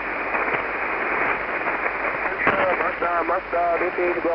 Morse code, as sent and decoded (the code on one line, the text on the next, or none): none